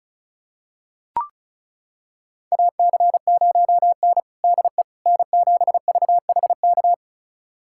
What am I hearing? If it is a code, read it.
Morse code, heard as a keyed tone, about 34 wpm, 700 Hz. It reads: AC0N DE N7VHK
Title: USA Callsign DE Callsign 35wpm